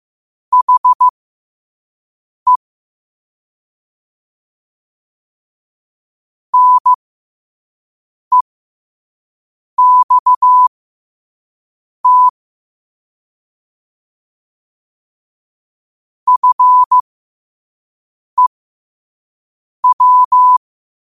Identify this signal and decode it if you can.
Morse code, heard as a keyed tone, about 15 wpm, spaced as for 5 wpm, 1000 Hz. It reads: HE NEXT FEW
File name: qotdfiles / 5wpm